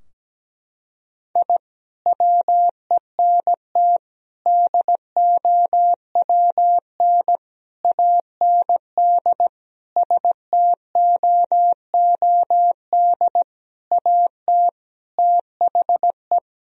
Morse code, heard as a keyed tone, about 17 wpm, 700 Hz. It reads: I WENT DOWN AND STOOD AT THE
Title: Sentences from Top 1000 Words - Review Part 2 - 17wpm